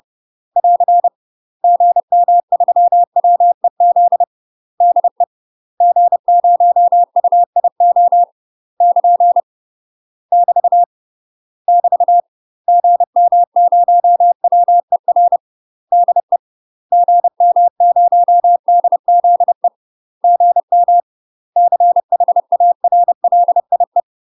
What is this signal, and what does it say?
Morse code, heard as a keyed tone, about 30 wpm, 700 Hz. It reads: <AR> GM3WEZ DE G0UIO <KN> <BT> <BT> GM0WER DE GM0DZE GM CHARLIE
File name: UK MASTER QSOs TRAINING / 30WPM / UK MASTER QSOs 0005